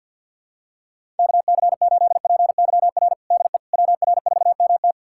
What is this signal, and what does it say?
Morse code, heard as a keyed tone, about 50 wpm, 700 Hz. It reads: KC8PYR DE WR4MT
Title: USA Callsign DE Callsign 50wpm